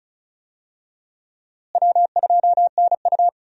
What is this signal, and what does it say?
Morse code, heard as a keyed tone, about 36 wpm, 700 Hz. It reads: W2NU
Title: USA Call Signs - 4 Characters - Repeated 3x - 35wpm